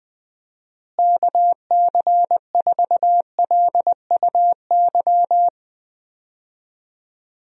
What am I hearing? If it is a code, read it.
Morse code, heard as a keyed tone, about 20 wpm, 700 Hz. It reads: KC4LUY
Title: USA Call Signs - 6 Characters - Repeated 3x - 20wpm